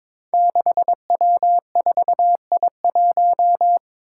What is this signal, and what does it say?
Morse code, heard as a keyed tone, about 22 wpm, 700 Hz. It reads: 6W4I1
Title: Five Challenging Letter-Number Combinations 22wpm